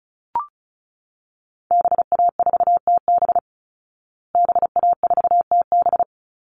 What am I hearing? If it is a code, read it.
Morse code, heard as a keyed tone, about 34 wpm, 700 Hz. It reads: BA4TB BA4TB
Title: Call Signs Repeated 3x - 35wpm